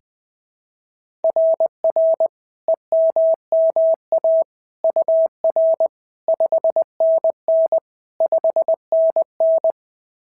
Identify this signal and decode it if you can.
Morse code, heard as a keyed tone, about 18 wpm, 650 Hz. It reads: RR EMMA UR 5NN 5NN
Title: SOTA QSO Practice 20wpm